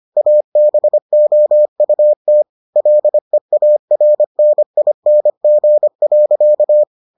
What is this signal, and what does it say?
Morse code, heard as a keyed tone, about 25 wpm, 600 Hz. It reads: ABOUT LEARNING.